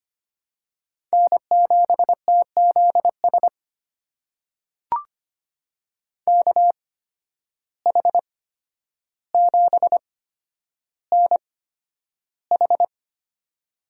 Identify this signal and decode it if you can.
Morse code, heard as a keyed tone, about 25 wpm, spaced as for 11 wpm, 700 Hz. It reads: N7TZS KH7NH
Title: USA Call Signs - 5 Characters - 8x Character Spacing - 25wpm